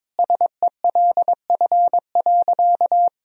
Morse code, heard as a keyed tone, about 21 wpm, 700 Hz. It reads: SELF.
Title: Sentences from Top 1000 Words - Review Part 1 - 22wpm